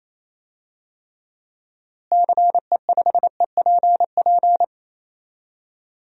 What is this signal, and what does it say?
Morse code, heard as a keyed tone, about 28 wpm, 700 Hz. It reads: CE5EPP
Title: DXCC-100 Call Signs - 6 Characters - Repeated 3x - 28wpm